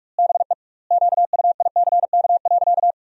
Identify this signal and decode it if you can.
Morse code, heard as a keyed tone, about 47 wpm, 700 Hz. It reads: BE QUICK.
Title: 2x Word-Spacing - Sentences from Top 500 Words - 45wpm